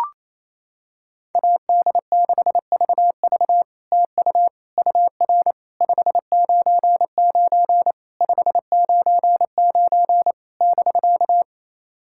Morse code, heard as a keyed tone, about 28 wpm, 700 Hz. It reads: AD6VV TU UR 599 599 <BK>
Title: SOTA QSO Practice 28wpm